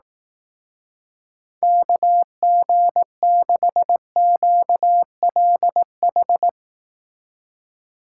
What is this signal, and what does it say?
Morse code, heard as a keyed tone, about 18 wpm, 700 Hz. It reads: KG6QLH
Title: Speed Racing - USA Call Signs 15wpm